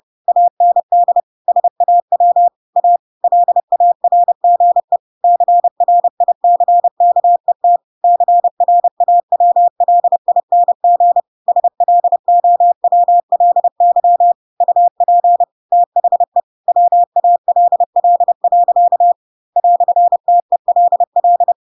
Morse code, heard as a keyed tone, about 30 wpm, 700 Hz. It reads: AND SAW A LARGE CRICKET CRAWLING SLOWLY UP THE WALL. "TELL
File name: Pinocchio / Pinocchio ch4 30 30 ik1hge